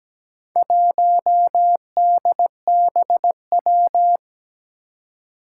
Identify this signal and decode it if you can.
Morse code, heard as a keyed tone, about 17 wpm, 700 Hz. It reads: 1DBW
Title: Four Challenging Letter-Number Combinations 17wpm